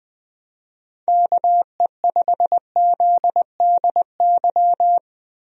Morse code, heard as a keyed tone, about 20 wpm, 700 Hz. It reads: KE5ZDY
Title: USA Call Signs - 6 Characters - Repeated 3x - 20wpm